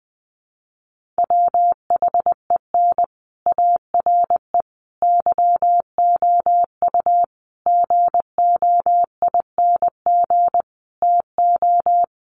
Morse code, heard as a keyed tone, about 20 wpm, 700 Hz. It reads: WHEN ARE YOU GOING TO
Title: Sets of 5 Words Encore2 20wpm